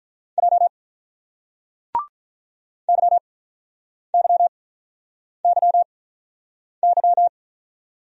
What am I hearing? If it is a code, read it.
Morse code, heard as a keyed tone, about 48 wpm, 700 Hz. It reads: J Y Y Y Y